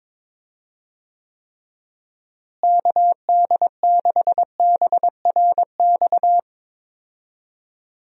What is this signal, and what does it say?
Morse code, heard as a keyed tone, about 22 wpm, 700 Hz. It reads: KD6BRX
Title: USA Call Signs - 6 Characters - Repeated 3x - 22wpm